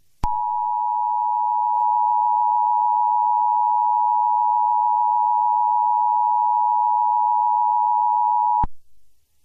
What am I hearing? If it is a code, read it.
Emergency Alert System (SAME) tones (the 853 and 960 Hz attention tone). It is heard from 0.2 to 8.7 s.